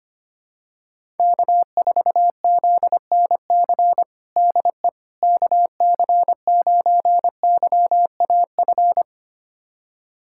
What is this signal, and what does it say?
Morse code, heard as a keyed tone, about 25 wpm, 700 Hz. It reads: K4ZNC DE KC9YAF
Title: USA Callsign DE Callsign 25wpm